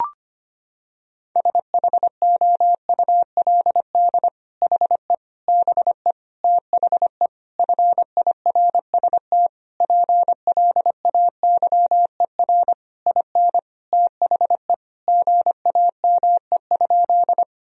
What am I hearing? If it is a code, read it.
Morse code, heard as a keyed tone, about 25 wpm, 700 Hz. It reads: SHOULD HE BE THE FIRST PLAYER IN THE GAME?
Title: Sentences from Top 1000 Words - Review Part 4 - 25wpm